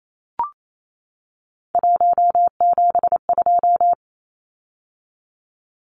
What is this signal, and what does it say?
Morse code, heard as a keyed tone, about 28 wpm, 700 Hz. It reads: 172
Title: Numbers - 3-Digit 28wpm